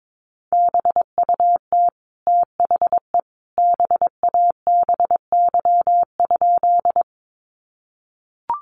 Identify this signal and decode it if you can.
Morse code, heard as a keyed tone, about 22 wpm, 700 Hz. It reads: BUT THE BABY?
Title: Sentences from Top 1300 Words 22wpm